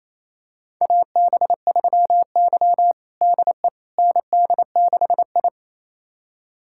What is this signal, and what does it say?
Morse code, heard as a keyed tone, about 28 wpm, 700 Hz. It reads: AB3Y DE ND6I
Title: USA Callsign DE Callsign - 4 Characters - 28wpm